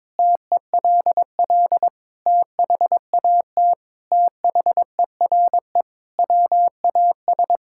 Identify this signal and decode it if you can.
Morse code, heard as a keyed tone, about 22 wpm, 700 Hz. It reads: TELL THAT THERE WAS
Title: Sentences from Top 1000 Words - Review Part 6 - 22wpm